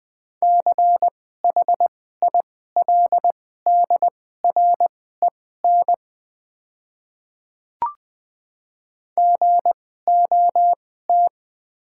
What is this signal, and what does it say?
Morse code, heard as a keyed tone, about 20 wpm, spaced as for 14 wpm, 700 Hz. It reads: CHILDREN GOT
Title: 2x Character Spacing - Top 300 Words - 20wpm